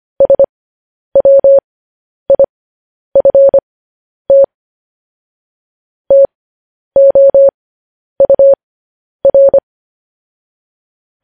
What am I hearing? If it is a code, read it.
Morse code, heard as a keyed tone, about 25 wpm, spaced as for 10 wpm, 550 Hz. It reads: SWIFT TOUR